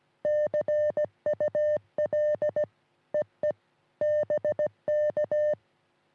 Morse code, heard as a keyed tone, about 17 wpm, 600 Hz. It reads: CUL EE BK
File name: sm7lcb / sm3lbn 3cm rs